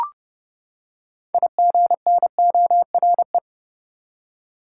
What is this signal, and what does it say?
Morse code, heard as a keyed tone, about 32 wpm, 700 Hz. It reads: IGNORE
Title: Six Letter Words - 30wpm